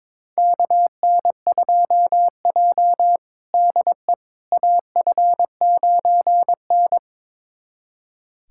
Morse code, heard as a keyed tone, about 22 wpm, 700 Hz. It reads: KN2J DE AF9N
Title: USA Callsign DE Callsign - 4 Characters - 22wpm